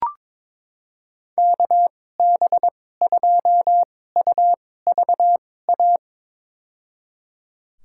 Morse code, heard as a keyed tone, about 22 wpm, spaced as for 16 wpm, 700 Hz. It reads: KB2UVA
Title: USA Call Signs - 6 Characters - 2x Character Spacing - 22wpm